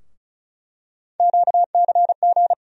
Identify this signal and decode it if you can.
Morse code, heard as a keyed tone, about 34 wpm, 700 Hz. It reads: QCG
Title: USA Call Sign Postfixes - 3 Letters 35wpm